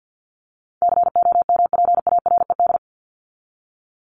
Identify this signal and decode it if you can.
Morse code, heard as a keyed tone, about 50 wpm, 700 Hz. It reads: COMPARED
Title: ED Suffix - Top 100 Words with Base Words 50wpm